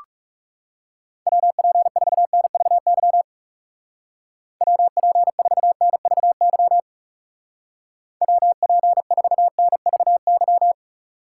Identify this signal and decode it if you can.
Morse code, heard as a keyed tone, about 35 wpm, 700 Hz. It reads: WP4NVY WP4NVY WP4NVY
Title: Speed Racing - USA 2x3 Call Signs 30wpm